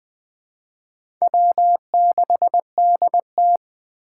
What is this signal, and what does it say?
Morse code, heard as a keyed tone, about 21 wpm, 700 Hz. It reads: W6DT
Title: 2023 SOTA Chaser Call Signs - 20wpm